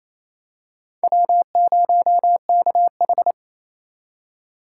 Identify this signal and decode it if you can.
Morse code, heard as a keyed tone, about 29 wpm, 700 Hz. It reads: W0KH
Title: USA Call Signs - 4 Characters - 4x Character Spacing - 28wpm